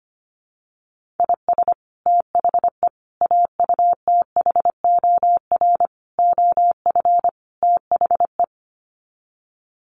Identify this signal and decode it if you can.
Morse code, heard as a keyed tone, about 25 wpm, 700 Hz. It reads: IS THE AUTHOR OF THE
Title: Sets of 5 Words Encore 25wpm